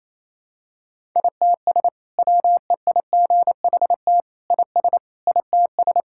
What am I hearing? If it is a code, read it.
Morse code, heard as a keyed tone, about 28 wpm, 700 Hz. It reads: ITS WEIGHT IS ITS
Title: Sentences from Top 1000 Words - Review Part 6 - 28wpm